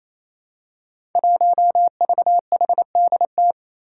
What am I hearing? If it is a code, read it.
Morse code, heard as a keyed tone, about 28 wpm, 700 Hz. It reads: 1VHDT
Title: Five Challenging Letter-Number Combinations 28wpm